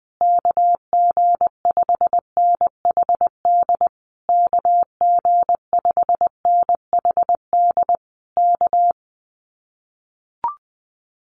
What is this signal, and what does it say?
Morse code, heard as a keyed tone, about 20 wpm, 700 Hz. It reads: KG5NHD KG5NHD K
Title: Common QSO Phrases Encore 20wpm